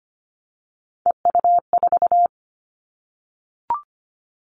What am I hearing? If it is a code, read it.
Morse code, heard as a keyed tone, about 25 wpm, 700 Hz. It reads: EU4
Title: Callsign Prefixes for Top 50 DXCC - 25wpm